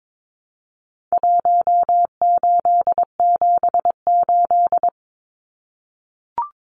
Morse code, heard as a keyed tone, about 22 wpm, 700 Hz. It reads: 1878
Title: Numbers - Contest Style - 4-Digit 22wpm